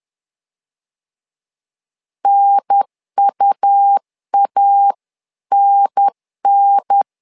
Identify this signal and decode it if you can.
Morse code, heard as a keyed tone, about 10 wpm, 800 Hz. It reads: NUANN
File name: LEZIONI / 2 UHDN / 2 UHDN 10wpm 1904